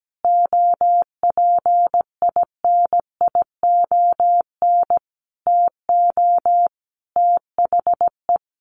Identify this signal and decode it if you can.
Morse code, heard as a keyed tone, about 17 wpm, 700 Hz. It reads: OPINION TO THE
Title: Sentences from Top 1300 Words 17wpm